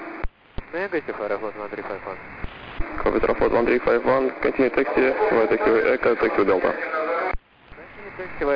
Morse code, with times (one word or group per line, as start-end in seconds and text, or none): none